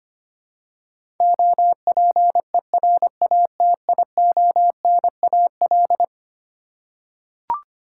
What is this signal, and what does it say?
Morse code, heard as a keyed tone, about 25 wpm, 700 Hz. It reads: OPERATIONAL
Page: morse-code-ninja-practice-25wpm